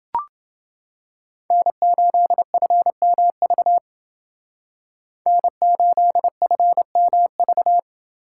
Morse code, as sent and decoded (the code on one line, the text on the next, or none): -. ---.. ..-. -- ...- / -. ---.. ..-. -- ...-
N8FMV N8FMV